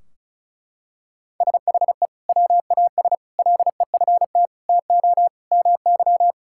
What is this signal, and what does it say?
Morse code, heard as a keyed tone, about 34 wpm, 700 Hz. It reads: SHE WAS LEFT TO MY
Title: Sentences from Top 1600 Words 35wpm